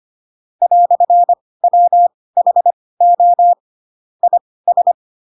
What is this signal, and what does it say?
Morse code, heard as a keyed tone, about 25 wpm, spaced as for 18 wpm, 700 Hz. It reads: "WHO IS
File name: Pinocchio / Pinocchio ch6 12 25 ik1hge